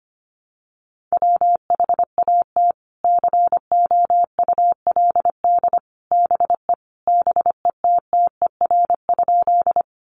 Morse code, heard as a keyed tone, about 25 wpm, 700 Hz. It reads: WHAT COULD BE BETTER?
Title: Sentences from Top 400 Words 25wpm